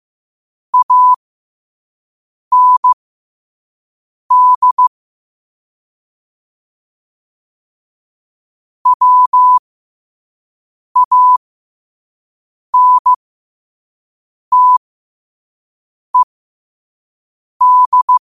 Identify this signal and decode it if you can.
Morse code, heard as a keyed tone, about 15 wpm, spaced as for 5 wpm, 1000 Hz. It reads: AND WANTED
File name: qotdfiles / 5wpm